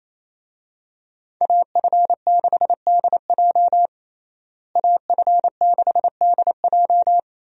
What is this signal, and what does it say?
Morse code, heard as a keyed tone, about 28 wpm, 700 Hz. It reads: AF6DJ AF6DJ
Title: USA Call Signs - 5 Characters - Repeated 3x - 28wpm